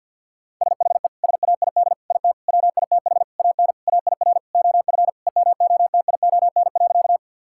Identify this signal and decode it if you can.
Morse code, heard as a keyed tone, about 50 wpm, 700 Hz. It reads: SHE SAID IT WITH AN AIR OF EMOTION.
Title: Sentences from Top 1600 Words 50wpm